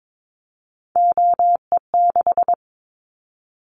Morse code, heard as a keyed tone, about 22 wpm, 700 Hz. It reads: OE6
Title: Callsign Prefixes for Top 50 DXCC - 22wpm